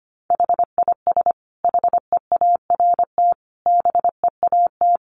Morse code, heard as a keyed tone, about 25 wpm, 700 Hz. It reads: HIS HEART BEAT
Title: Sentences from Top 1300 Words 25wpm